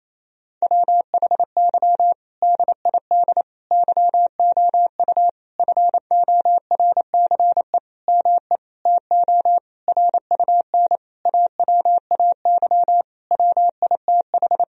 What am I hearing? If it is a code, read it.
Morse code, heard as a keyed tone, about 27 wpm, 700 Hz. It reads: WHY DID YOU FORCE ME TO RUN AWAY WITH
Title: Sentences from Top 1000 Words - Review Part 3 - 28wpm